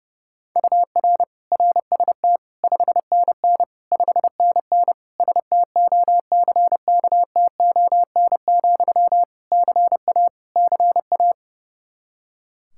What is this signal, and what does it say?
Morse code, heard as a keyed tone, about 32 wpm, 700 Hz. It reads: UR RST 5NN 5NN STOCKTON, CA CA
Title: Practice QSO - Exchange 1 - Beginning - 30wpm